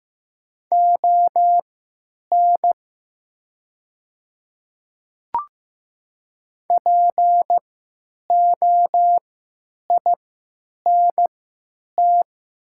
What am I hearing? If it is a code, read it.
Morse code, heard as a keyed tone, about 15 wpm, spaced as for 9 wpm, 700 Hz. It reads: ON POINT
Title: 3x Character Spacing - Top 200 Words - 15wpm